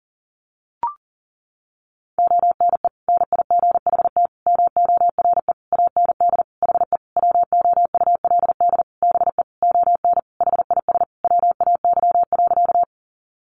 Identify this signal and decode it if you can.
Morse code, heard as a keyed tone, about 40 wpm, 700 Hz. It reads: ONE NIGHT MORE AND HE WOULD BE ON HIS WAY.
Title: Sentences from Top 300 Words 40wpm